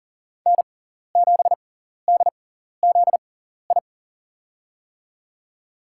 Morse code, heard as a keyed tone, about 40 wpm, spaced as for 14 wpm, 700 Hz. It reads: N7DZI
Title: USA Call Signs - 5 Characters - 6x Character Spacing - 40wpm